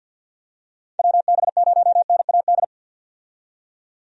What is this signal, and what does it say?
Morse code, heard as a keyed tone, about 53 wpm, 700 Hz. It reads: WB0NAD
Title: USA Call Signs - 6 Characters - 50wpm